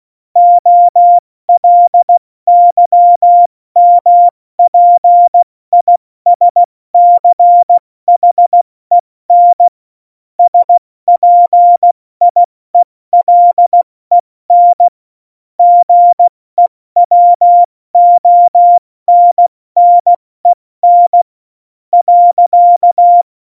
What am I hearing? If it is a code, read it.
Morse code, heard as a keyed tone, about 16 wpm, spaced as for 14 wpm, 700 Hz. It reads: OLYMPISCHEN SPIELEN GEWONNEN.
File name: Morsekurs / Witze / Witz 01. Dezember 2020, 70 F80